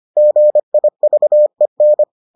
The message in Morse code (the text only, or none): GIVEN